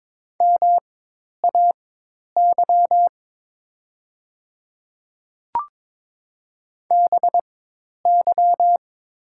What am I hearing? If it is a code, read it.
Morse code, heard as a keyed tone, about 22 wpm, spaced as for 10 wpm, 700 Hz. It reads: MAY BY
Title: 4x Character Spacing - Top 200 Words - 22wpm